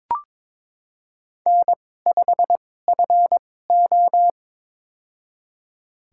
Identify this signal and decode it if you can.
Morse code, heard as a keyed tone, about 22 wpm, spaced as for 16 wpm, 700 Hz. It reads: N5FO
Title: USA Call Signs - 4 Characters - 2x Character Spacing - 22wpm